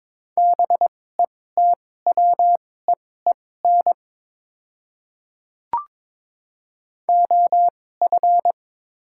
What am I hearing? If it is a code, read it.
Morse code, heard as a keyed tone, about 22 wpm, spaced as for 16 wpm, 700 Hz. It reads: BETWEEN OF
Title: 2x Character Spacing - Top 500 Words - 22wpm